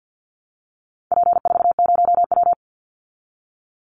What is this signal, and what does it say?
Morse code, heard as a keyed tone, about 48 wpm, 700 Hz. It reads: P40W
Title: Call Signs Repeated 2x - 50wpm